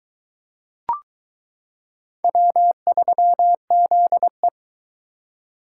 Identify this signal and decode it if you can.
Morse code, heard as a keyed tone, about 22 wpm, 700 Hz. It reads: W3ZE